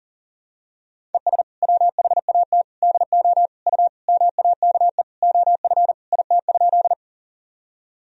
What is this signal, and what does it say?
Morse code, heard as a keyed tone, about 40 wpm, 700 Hz. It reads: ES WHAT DO U MAKE OF IT?
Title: Sentences from Top 100 Words and Abbreviations 40wpm